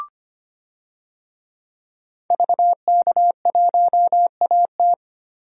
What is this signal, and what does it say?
Morse code, heard as a keyed tone, about 25 wpm, 700 Hz. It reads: VK1AT
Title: DXCC-100 Call Signs - 5 Characters - 25wpm